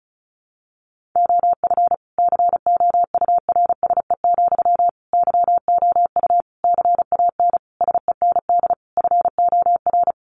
Morse code, heard as a keyed tone, about 34 wpm, 700 Hz. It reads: OF COURSE, YOU CAN SEND FOR
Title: Sentences from Top 1500 Words 35wpm